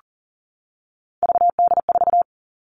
Morse code, heard as a keyed tone, about 40 wpm, 700 Hz. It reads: VD4
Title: All Cut Numbers - 3-Digit 40wpm